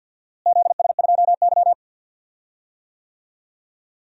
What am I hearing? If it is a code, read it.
Morse code, heard as a keyed tone, about 52 wpm, 700 Hz. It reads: ZS2Y
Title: DXCC-100 Call Signs - 4 Characters - Repeated 2x - 50wpm